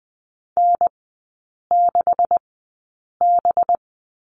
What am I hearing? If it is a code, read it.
Morse code, heard as a keyed tone, about 20 wpm, spaced as for 8 wpm, 690 Hz. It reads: N6B
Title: 191 Lesson 31e - Letter v - Three Characters 20wpm